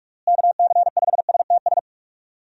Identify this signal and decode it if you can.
Morse code, heard as a keyed tone, about 46 wpm, 700 Hz. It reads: KC5STS
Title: USA Call Signs - 6 Characters - Repeated 3x - 45wpm